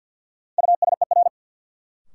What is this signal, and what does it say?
Morse code, heard as a keyed tone, about 52 wpm, 700 Hz. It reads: USER
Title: Sentences from Top 1000 Words - Review Part 5 - New Word Drill - 50wpm